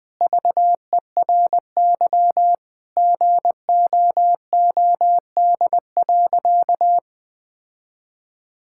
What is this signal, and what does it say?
Morse code, heard as a keyed tone, about 19 wpm, 700 Hz. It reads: VERY GOOD.
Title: Sentences from Top 1000 Words - Review Part 5 - 20wpm